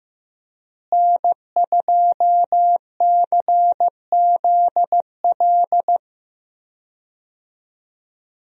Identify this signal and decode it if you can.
Morse code, heard as a keyed tone, about 15 wpm, 700 Hz. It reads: N2CZL
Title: USA Call Signs - 5 Characters - Repeated 2x - 15wpm